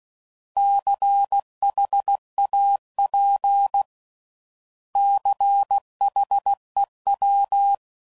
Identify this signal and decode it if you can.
Morse code, heard as a keyed tone, about 16 wpm, 800 Hz. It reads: CHAP CHEW
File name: Prefixes / By First Letter / Cs